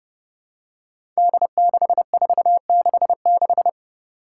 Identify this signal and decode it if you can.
Morse code, heard as a keyed tone, about 30 wpm, 700 Hz. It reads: D6466
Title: Five Challenging Letter-Number Combinations 30wpm